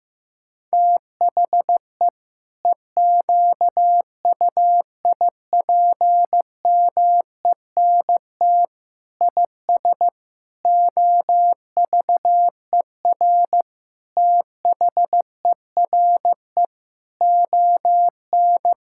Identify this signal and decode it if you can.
Morse code, heard as a keyed tone, about 16 wpm, 700 Hz. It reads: THE EQUIPMENT IS OVER THERE ON
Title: Sentences from Top 1000 Words - Review Part 5 - 15wpm